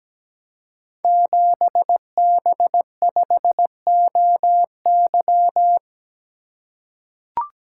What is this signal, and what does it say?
Morse code, heard as a keyed tone, about 17 wpm, 700 Hz. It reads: 7B5OY
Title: Code Groups - 5 Letter:Numbers - 17wpm